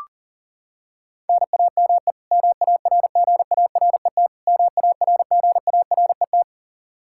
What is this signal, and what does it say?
Morse code, heard as a keyed tone, about 40 wpm, 700 Hz. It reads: NAME MARGARET MARGARET
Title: Practice QSO - Exchange 1 - Beginning - 40wpm